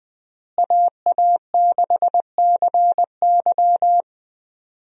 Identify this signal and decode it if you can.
Morse code, heard as a keyed tone, about 20 wpm, 700 Hz. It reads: AA6CY
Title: USA Call Signs - 5 Characters - Repeated 3x - 20wpm